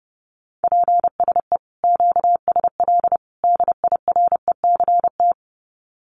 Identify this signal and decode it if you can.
Morse code, heard as a keyed tone, about 30 wpm, 700 Hz. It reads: PSE QSL DIRECT
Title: Common QSO Phrases 30wpm